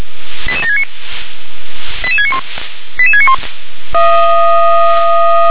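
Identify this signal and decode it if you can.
Five-tone selective call (ZVEI), heard as quick runs of five tones, tones of about 70 ms. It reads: 88769 88691 88691
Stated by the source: ZVEI five-tone call received by SDR.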